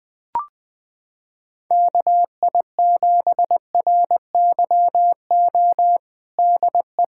Morse code, heard as a keyed tone, about 20 wpm, 700 Hz. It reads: KI7RYO DE
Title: USA Callsign DE Callsign - 6 Characters - 20wpm